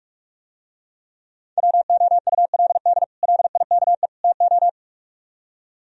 Morse code, heard as a keyed tone, about 45 wpm, 700 Hz. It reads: WOULD LIKE TO